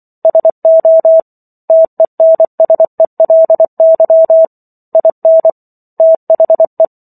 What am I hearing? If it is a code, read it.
Morse code, heard as a keyed tone, about 24 wpm, 650 Hz. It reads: SO TENSELY IN THE